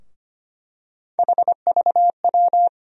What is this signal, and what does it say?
Morse code, heard as a keyed tone, about 25 wpm, 700 Hz. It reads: HVW
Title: USA Call Sign Postfixes - 3 Letters 25wpm